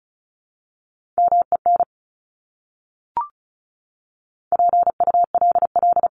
Morse code, heard as a keyed tone, about 34 wpm, 700 Hz. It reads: MEN PULL
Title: Top 500 Words Repeated 35wpm